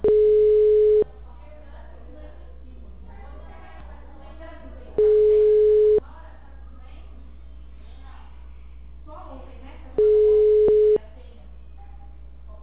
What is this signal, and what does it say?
Ringback tone (Europe, 425 Hz), with speech.
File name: out- -2020- - - .9379